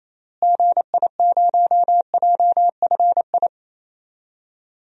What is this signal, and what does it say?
Morse code, heard as a keyed tone, about 28 wpm, 700 Hz. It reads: GI0JFI
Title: DXCC-100 Call Signs - 6 Characters - Repeated 3x - 28wpm